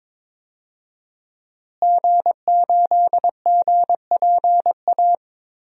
Morse code, heard as a keyed tone, about 22 wpm, 700 Hz. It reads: G8GPA
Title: DXCC-100 Call Signs - 5 Characters - Repeated 2x - 22wpm